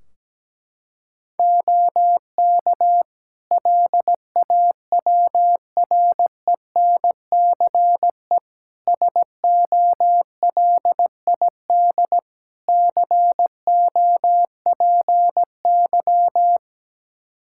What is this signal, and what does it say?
Morse code, heard as a keyed tone, about 17 wpm, 700 Hz. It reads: OK LAWRENCE SOLID COPY
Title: Practice QSO - Exchange 3 - More Info - 17wpm